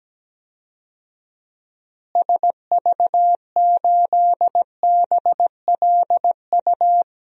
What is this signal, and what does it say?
Morse code, heard as a keyed tone, about 17 wpm, 700 Hz. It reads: SV8BLU
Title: DXCC-100 Call Signs - 6 Characters - Repeated 2x - 17wpm